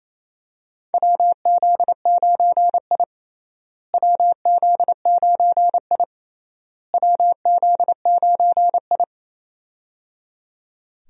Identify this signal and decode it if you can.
Morse code, heard as a keyed tone, about 28 wpm, 700 Hz. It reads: WZ9I WZ9I WZ9I
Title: USA Call Signs - 4 Characters - Repeated 3x - 28wpm